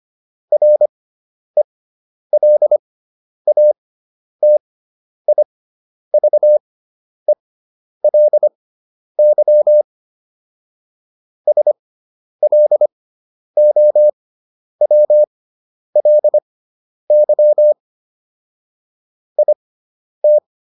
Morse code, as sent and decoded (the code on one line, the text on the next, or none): .-. .-.. .- - .. ...- . .-.. -.-- / ... .-.. --- .-- .-.. -.-- / .. -
RELATIVELY SLOWLY IT